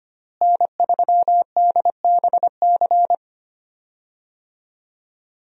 Morse code, heard as a keyed tone, about 25 wpm, 700 Hz. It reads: N3DBC
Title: USA Call Signs - 5 Characters - Repeated 2x - 25wpm